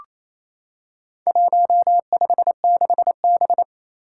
Morse code, heard as a keyed tone, about 28 wpm, 700 Hz. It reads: 156B
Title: Four Challenging Letter-Number Combinations 28wpm